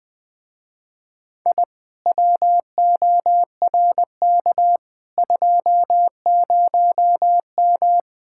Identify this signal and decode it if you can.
Morse code, heard as a keyed tone, about 20 wpm, 700 Hz. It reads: I WORK 20M
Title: Speed-Racing Common QSO Phrases - 17wpm